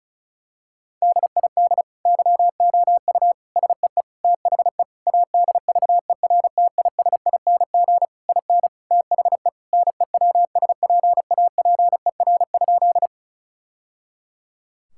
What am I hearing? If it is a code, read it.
Morse code, heard as a keyed tone, about 36 wpm, 700 Hz. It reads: DID YOU SEE THE ADVERTISING IN THE NEWSPAPER?
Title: Sentences from Top 1000 Words - Review Part 6 - 35wpm